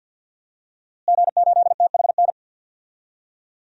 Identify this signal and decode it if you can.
Morse code, heard as a keyed tone, about 50 wpm, 700 Hz. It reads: G8THN
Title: DXCC-100 Call Signs - 5 Characters - Repeated 2x - 50wpm